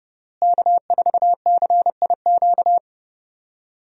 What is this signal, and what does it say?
Morse code, heard as a keyed tone, about 30 wpm, 700 Hz. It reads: K4CIQ
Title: USA Call Signs - 5 Characters - Repeated 3x - 30wpm